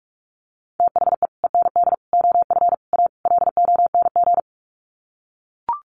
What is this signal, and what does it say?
Morse code, heard as a keyed tone, about 44 wpm, 700 Hz. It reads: THE END OF A LONG
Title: Sets of 5 Words Encore2 45wpm